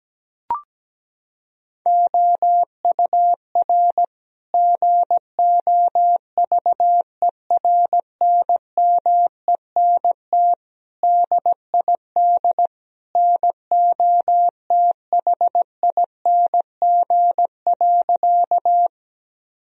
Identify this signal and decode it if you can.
Morse code, heard as a keyed tone, about 17 wpm, 700 Hz. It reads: OUR GOVERNMENT DID NOTHING.
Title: Sentences from Top 1000 Words - Review Part 3 - 17wpm